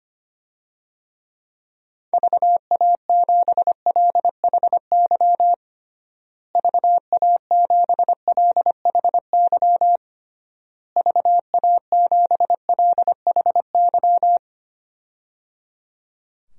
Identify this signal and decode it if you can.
Morse code, heard as a keyed tone, about 25 wpm, 700 Hz. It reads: VA7LHY VA7LHY VA7LHY
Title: DXCC-100 Call Signs - 6 Characters - Repeated 3x - 25wpm